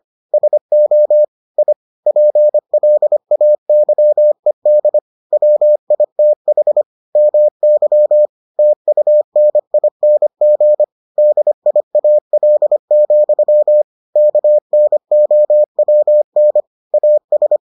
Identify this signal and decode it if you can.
Morse code, heard as a keyed tone, about 25 wpm, 600 Hz. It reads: SO I PLAYED WITH MY TUNING DIAL, KNOWN AS